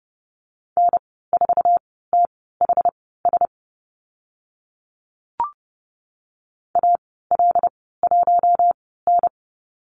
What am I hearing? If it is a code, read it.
Morse code, heard as a keyed tone, about 30 wpm, spaced as for 17 wpm, 700 Hz. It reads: N4THS AL1N
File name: Lesson 14g - Number 2 - Call Signs 30wpm